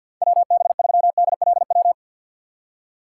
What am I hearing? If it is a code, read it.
Morse code, heard as a keyed tone, about 48 wpm, 700 Hz. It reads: WB3DLW
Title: USA Call Signs - 3x Character Spacing - 50wpm